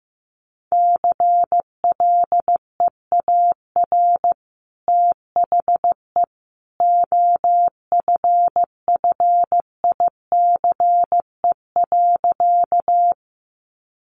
Morse code, heard as a keyed tone, about 15 wpm, 700 Hz. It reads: CLEAR THE OFFICE.
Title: Sentences from Top 1000 Words 15wpm